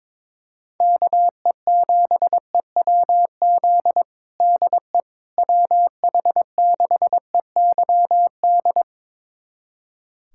Morse code, heard as a keyed tone, about 22 wpm, 700 Hz. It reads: KE7EWZ DE WH6EYD
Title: USA Callsign DE Callsign 22wpm